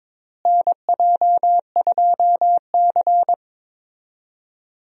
Morse code, heard as a keyed tone, about 22 wpm, 700 Hz. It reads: NJ2C
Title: USA Call Signs - 4 Characters - Repeated 3x - 22wpm